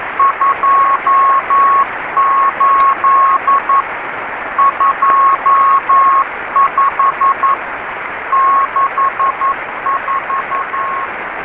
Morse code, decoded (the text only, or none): 28 25 64